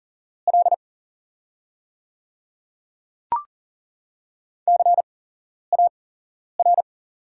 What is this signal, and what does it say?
Morse code, heard as a keyed tone, about 40 wpm, spaced as for 11 wpm, 690 Hz. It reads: L CAR
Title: 8x Character Spacing - Top 300 Words - 40wpm